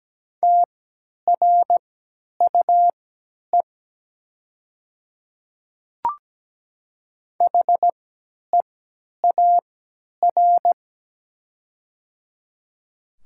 Morse code, heard as a keyed tone, about 17 wpm, spaced as for 10 wpm, 700 Hz. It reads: TRUE HEAR
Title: 3x Character Spacing - Top 500 Words - 17wpm